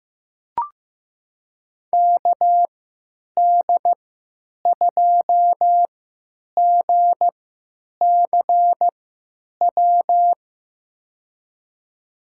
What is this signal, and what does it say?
Morse code, heard as a keyed tone, about 15 wpm, spaced as for 9 wpm, 700 Hz. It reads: KD2GCW
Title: USA Call Signs - 6 Characters - 3x Character Spacing - 15wpm